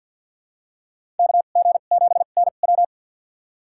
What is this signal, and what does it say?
Morse code, heard as a keyed tone, about 51 wpm, spaced as for 36 wpm, 700 Hz. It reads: KG7NW
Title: USA Call Signs - 5 Characters - 2x Character Spacing - 50wpm